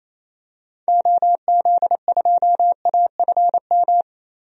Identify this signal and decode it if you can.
Morse code, heard as a keyed tone, about 28 wpm, 700 Hz. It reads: OZ2AFM
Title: DXCC-100 Call Signs - 6 Characters - Repeated 2x - 28wpm